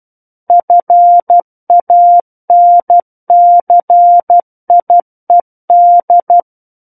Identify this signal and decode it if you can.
Morse code, heard as a keyed tone, about 12 wpm, 700 Hz. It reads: FANCIED
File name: Book One / 03-Chapter 3 - 12 WPM